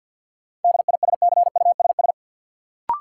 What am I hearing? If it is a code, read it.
Morse code, heard as a keyed tone, about 46 wpm, 700 Hz. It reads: DISCUSS
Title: Sentences from Top 1000 Words - New Word Drill - 50wpm